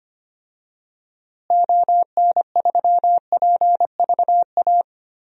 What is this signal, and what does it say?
Morse code, heard as a keyed tone, about 25 wpm, 700 Hz. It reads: ON3PVA
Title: DXCC-100 Call Signs - 6 Characters - Repeated 2x - 25wpm